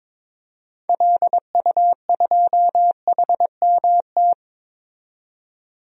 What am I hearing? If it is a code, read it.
Morse code, heard as a keyed tone, about 22 wpm, 700 Hz. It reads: LU2HMT
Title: DXCC-100 Call Signs - 6 Characters - 22wpm